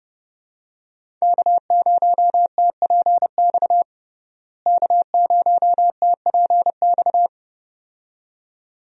Morse code, heard as a keyed tone, about 30 wpm, 700 Hz. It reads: K0TPX K0TPX
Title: USA Call Signs - 5 Characters - Repeated 2x - 30wpm